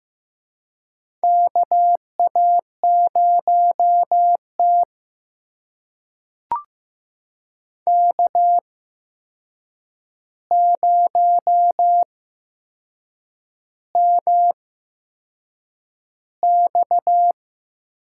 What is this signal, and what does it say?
Morse code, heard as a keyed tone, about 15 wpm, spaced as for 6 wpm, 700 Hz. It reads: KA0T K0MX
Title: USA Call Signs - 4 Characters - 8x Character Spacing - 15wpm